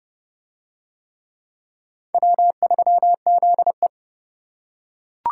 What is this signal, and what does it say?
Morse code, heard as a keyed tone, about 28 wpm, 700 Hz. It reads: W3ZE